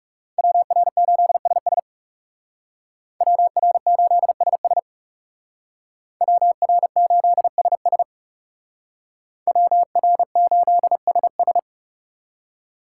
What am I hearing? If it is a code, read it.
Morse code, heard as a keyed tone, about 38 wpm, 700 Hz. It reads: WR8SS WR8SS WR8SS WR8SS